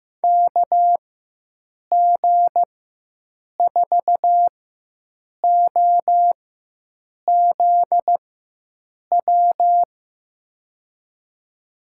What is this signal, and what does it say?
Morse code, heard as a keyed tone, about 15 wpm, spaced as for 7 wpm, 700 Hz. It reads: KG4OZW
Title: USA Call Signs - 6 Characters - 4x Character Spacing - 15wpm